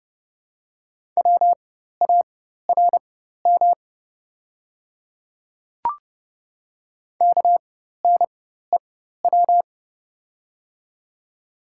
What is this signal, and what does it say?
Morse code, heard as a keyed tone, about 30 wpm, spaced as for 14 wpm, 700 Hz. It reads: WARM KNEW
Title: 4x Character Spacing - Top 500 Words - 30wpm